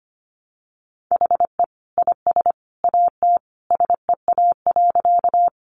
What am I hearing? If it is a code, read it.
Morse code, heard as a keyed tone, about 25 wpm, 700 Hz. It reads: HE IS AT SEA.
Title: Sentences from Top 300 Words 25wpm